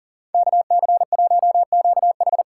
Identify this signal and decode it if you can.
Morse code, heard as a keyed tone, about 49 wpm, 700 Hz. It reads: KC1QH
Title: Speed Racing - USA Call Signs 40wpm